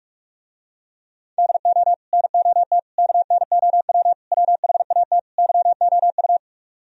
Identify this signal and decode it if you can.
Morse code, heard as a keyed tone, about 47 wpm, 700 Hz. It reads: DO NOT KNOW WHAT YOU